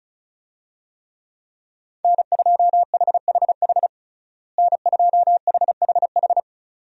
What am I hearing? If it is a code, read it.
Morse code, heard as a keyed tone, about 36 wpm, 700 Hz. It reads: N2HHH N2HHH
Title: USA Call Signs - 5 Characters - Repeated 2x - 35wpm